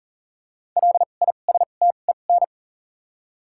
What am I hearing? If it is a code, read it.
Morse code, heard as a keyed tone, about 40 wpm, 700 Hz. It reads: L I S T E N
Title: 2x Character Spacing - Top 500 Words - 40wpm